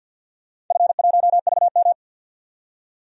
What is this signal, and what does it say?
Morse code, heard as a keyed tone, about 53 wpm, 700 Hz. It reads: F1VM
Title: DXCC-100 Call Signs - 4 Characters - 50wpm